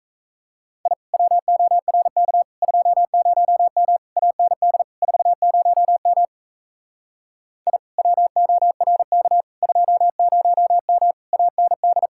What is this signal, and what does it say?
Morse code, heard as a keyed tone, about 37 wpm, 700 Hz. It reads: I WORK 20M AND 40M I WORK 20M AND